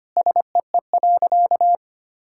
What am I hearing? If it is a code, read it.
Morse code, heard as a keyed tone, about 25 wpm, 700 Hz. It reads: SEE.
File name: practice / 25wpm / Sentences with 3-Word Phrases - 25wpm